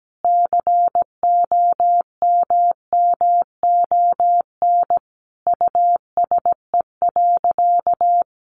Morse code, heard as a keyed tone, about 17 wpm, 700 Hz. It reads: COMMON USE.
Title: Sentences from Top 1300 Words 17wpm